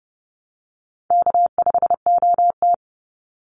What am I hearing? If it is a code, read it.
Morse code, heard as a keyed tone, about 30 wpm, 700 Hz. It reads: K5OT
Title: NA SOTA Chaser Callsigns 30wpm